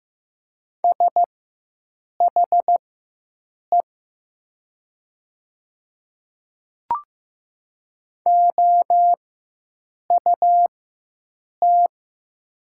Morse code, heard as a keyed tone, about 15 wpm, spaced as for 7 wpm, 700 Hz. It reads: SHE OUT
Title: 4x Character Spacing - Top 500 Words - 15wpm